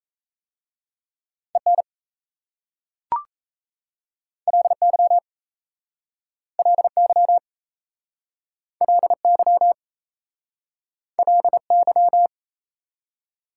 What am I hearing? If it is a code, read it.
Morse code, heard as a keyed tone, about 37 wpm, 700 Hz. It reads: EN LY LY LY LY